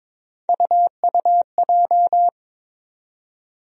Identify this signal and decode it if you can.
Morse code, heard as a keyed tone, about 22 wpm, 700 Hz. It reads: UUJ
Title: USA Call Sign Postfixes - 3 Letters 22wpm